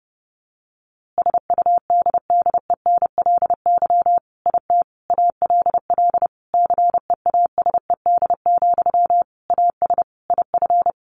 Morse code, heard as a keyed tone, about 30 wpm, 700 Hz. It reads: SUDDENLY IT ALL CEASED, AS IF
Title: A Sound Of Thunder 30wpm